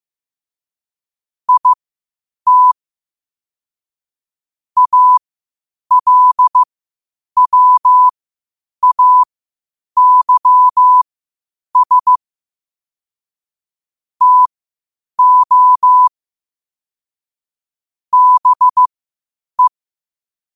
Morse code, heard as a keyed tone, about 15 wpm, spaced as for 8 wpm, 1000 Hz. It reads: IT ALWAYS TO BE
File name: qotdfiles / 8wpm